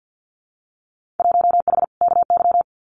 Morse code, heard as a keyed tone, about 50 wpm, 700 Hz. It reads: 1H KY
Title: Winter Field Day Exchanges 50wpm